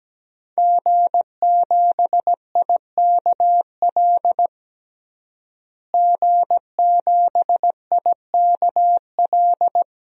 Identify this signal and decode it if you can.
Morse code, heard as a keyed tone, about 17 wpm, 700 Hz. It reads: G7IKL G7IKL
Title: DXCC-100 Call Signs - 5 Characters - Repeated 2x - 17wpm